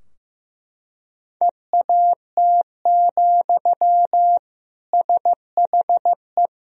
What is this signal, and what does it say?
Morse code, heard as a keyed tone, about 15 wpm, 700 Hz. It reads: EAT, SHE
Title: Sentences from Top 1000 Words - Review Part 2 - 15wpm